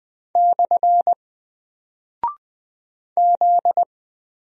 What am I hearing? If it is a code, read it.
Morse code, heard as a keyed tone, about 20 wpm, 700 Hz. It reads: / Z
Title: Single Character (Letter:Number:Punctuation:Prosign) 20wpm